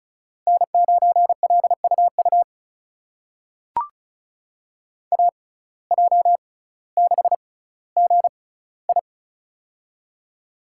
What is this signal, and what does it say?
Morse code, heard as a keyed tone, about 36 wpm, spaced as for 18 wpm, 700 Hz. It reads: N9LUU AJ6GI
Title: USA Call Signs - 5 Characters - 6x Character Spacing - 35wpm